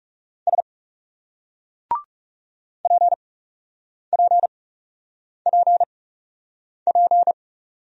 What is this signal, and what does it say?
Morse code, heard as a keyed tone, about 42 wpm, spaced as for 8 wpm, 700 Hz. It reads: S PPPP